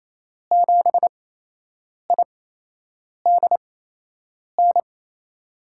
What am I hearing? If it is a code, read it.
Morse code, heard as a keyed tone, about 27 wpm, spaced as for 8 wpm, 700 Hz. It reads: 7IDN
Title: USA Call Signs - 8x Character Spacing - 28wpm